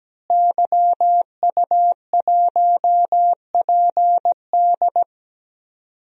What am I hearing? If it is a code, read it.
Morse code, heard as a keyed tone, about 17 wpm, 700 Hz. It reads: YU1PD
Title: DXCC-100 Call Signs - 5 Characters - Repeated 2x - 17wpm